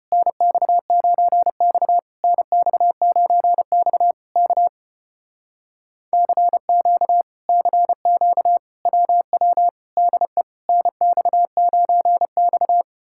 Morse code, heard as a keyed tone, about 33 wpm, 700 Hz. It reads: NX9X NX9X K CQ CQ WW DE NX9X